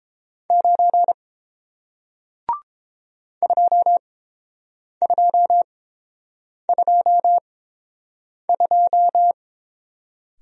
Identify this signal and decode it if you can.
Morse code, heard as a keyed tone, about 30 wpm, spaced as for 7 wpm, 700 Hz. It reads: 9 2222